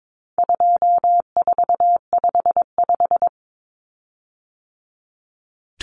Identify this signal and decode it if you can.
Morse code, heard as a keyed tone, about 22 wpm, 700 Hz. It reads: 2455
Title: Common Cut Numbers - 4-Digit 22wpm